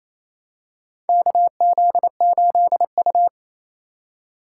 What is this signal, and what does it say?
Morse code, heard as a keyed tone, about 28 wpm, 700 Hz. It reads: KZ8U
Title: USA Call Signs - 4 Characters - 3x Character Spacing - 28wpm